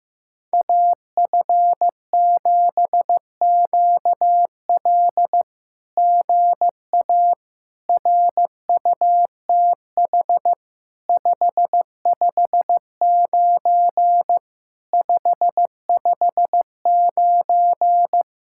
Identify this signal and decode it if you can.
Morse code, heard as a keyed tone, about 15 wpm, 700 Hz. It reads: AF7QL GA RUTH 559 559
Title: SOTA QSO Practice 15wpm